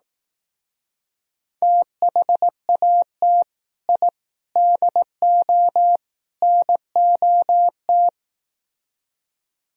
Morse code, heard as a keyed tone, about 18 wpm, 700 Hz. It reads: THAT I DO NOT
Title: Speed-Racing - Sets of 4 Words 15wpm